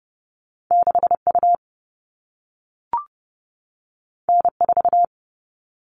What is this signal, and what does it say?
Morse code, heard as a keyed tone, about 30 wpm, 700 Hz. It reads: 6U N4
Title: All Cut Numbers - 2-Digit 30wpm